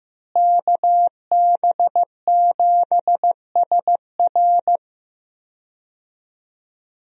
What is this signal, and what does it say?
Morse code, heard as a keyed tone, about 15 wpm, 700 Hz. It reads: KB7SR
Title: USA Call Signs - 5 Characters - Repeated 3x - 15wpm